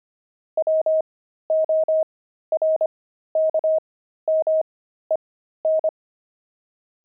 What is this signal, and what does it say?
Morse code, heard as a keyed tone, about 25 wpm, spaced as for 13 wpm, 630 Hz. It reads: WORKMEN
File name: pangram-fs-13-cs-25-steaks